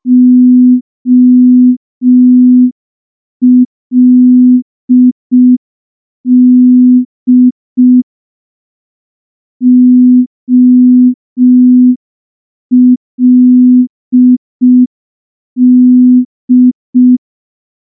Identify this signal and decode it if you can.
Morse code, heard as a keyed tone, about 5 wpm, 250 Hz. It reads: OLD OLD
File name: morse / morse-training-05WPM